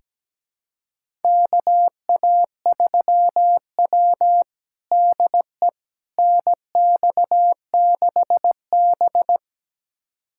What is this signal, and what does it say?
Morse code, heard as a keyed tone, about 17 wpm, 700 Hz. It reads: KA3W DE NX6B
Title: USA Callsign DE Callsign - 4 Characters - 17wpm